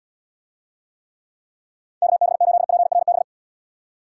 Morse code, heard as a keyed tone, about 79 wpm, 700 Hz. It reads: KB8PRB